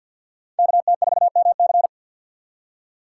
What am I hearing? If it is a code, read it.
Morse code, heard as a keyed tone, about 53 wpm, 700 Hz. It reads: KT4MC
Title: USA Call Signs - 5 Characters - 4x Character Spacing - 50wpm